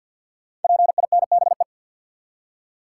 Morse code, heard as a keyed tone, about 48 wpm, 700 Hz. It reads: PINBE
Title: Code Groups - 5 Letter:Numbers - 50wpm